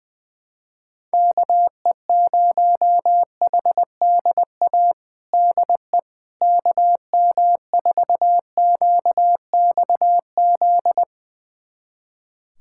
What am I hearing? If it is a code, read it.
Morse code, heard as a keyed tone, about 21 wpm, 700 Hz. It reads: KE0HDA DE KM4QXZ
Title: USA Callsign DE Callsign - 6 Characters - 20wpm